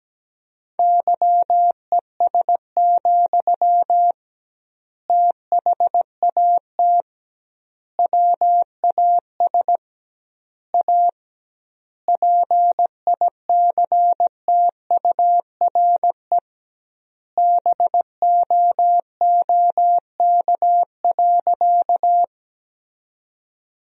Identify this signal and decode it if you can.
Morse code, heard as a keyed tone, about 17 wpm, 700 Hz. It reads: YES, THAT WAS A PICTURE BOOK.
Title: 2x Word-Spacing - Sentences from Top 500 Words - 17wpm